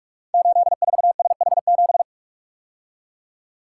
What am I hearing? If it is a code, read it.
Morse code, heard as a keyed tone, about 44 wpm, 700 Hz. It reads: 84SH7
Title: Code Groups - 5 Letter:Numbers - 45wpm